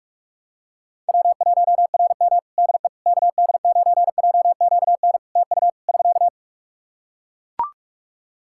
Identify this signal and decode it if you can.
Morse code, heard as a keyed tone, about 43 wpm, 700 Hz. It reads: W1RM DE KD9JQN TU <SK>
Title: K1USN SST QSO Practice 45wpm